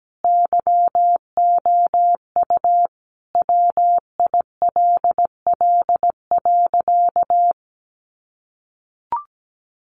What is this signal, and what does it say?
Morse code, heard as a keyed tone, about 17 wpm, 700 Hz. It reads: YOU WILL.
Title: Sentences from Top 1000 Words 17wpm